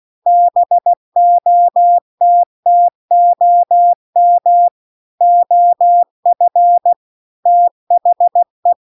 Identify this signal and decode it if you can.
Morse code, heard as a keyed tone, about 16 wpm, 700 Hz. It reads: BOTTOM OF THE